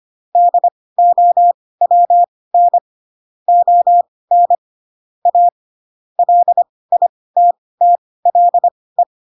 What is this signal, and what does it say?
Morse code, heard as a keyed tone, about 25 wpm, spaced as for 18 wpm, 700 Hz. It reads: DOWN ON A LITTLE
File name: Pinocchio / Pinocchio ch6 12 25 ik1hge